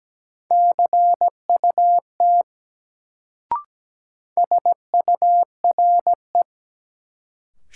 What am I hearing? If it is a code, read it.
Morse code, heard as a keyed tone, about 17 wpm, 700 Hz. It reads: CUT SURE
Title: Sentences from Top 300 Words - New Word Drill - 17wpm